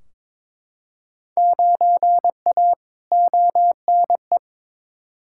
Morse code, heard as a keyed tone, about 22 wpm, 700 Hz. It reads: 9A ONE
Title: Field Day Exchanges Encore - 22wpm